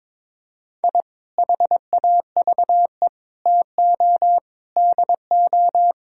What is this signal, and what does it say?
Morse code, heard as a keyed tone, about 22 wpm, 700 Hz. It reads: I HAVE TO DO
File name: practice / 22wpm / Sentences with 3-Word Phrases - 22wpm